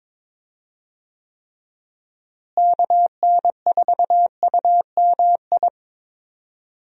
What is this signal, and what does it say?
Morse code, heard as a keyed tone, about 22 wpm, 700 Hz. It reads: KN4UMI
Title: USA Call Signs - 6 Characters - Repeated 3x - 22wpm